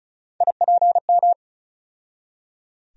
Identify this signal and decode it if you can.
Morse code, heard as a keyed tone, about 34 wpm, 700 Hz. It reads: IPM
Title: USA Call Sign Postfixes - 3 Letters 35wpm